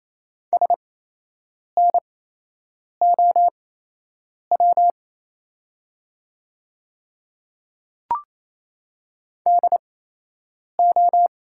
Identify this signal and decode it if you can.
Morse code, heard as a keyed tone, about 28 wpm, spaced as for 8 wpm, 700 Hz. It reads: SNOW DO
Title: 8x Character Spacing - Top 500 Words - 28wpm